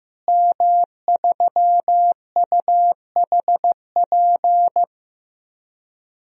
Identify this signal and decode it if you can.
Morse code, heard as a keyed tone, about 15 wpm, 700 Hz. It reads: M3UHP
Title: DXCC-100 Call Signs - 5 Characters - Repeated 2x - 15wpm